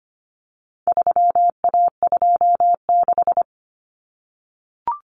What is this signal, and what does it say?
Morse code, heard as a keyed tone, about 25 wpm, 700 Hz. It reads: 3A26
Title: Common Cut Numbers - 4-Digit 25wpm